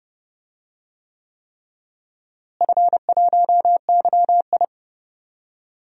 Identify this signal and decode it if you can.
Morse code, heard as a keyed tone, about 30 wpm, 700 Hz. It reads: F1YI
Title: DXCC-100 Call Signs - 4 Characters - Repeated 3x - 30wpm